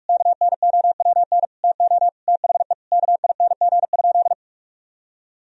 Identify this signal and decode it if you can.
Morse code, heard as a keyed tone, about 46 wpm, 700 Hz. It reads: KNOWN TO THE KING?
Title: Sentences from Top 1000 Words - Review Part 4 - 45wpm